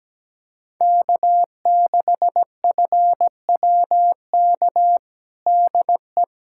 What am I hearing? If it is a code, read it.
Morse code, heard as a keyed tone, about 17 wpm, 700 Hz. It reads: K6FWK DE
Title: USA Callsign DE Callsign 17wpm